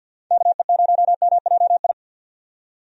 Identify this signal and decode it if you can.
Morse code, heard as a keyed tone, about 45 wpm, 700 Hz. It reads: KE0MJI
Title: USA Call Signs - 6 Characters - 50wpm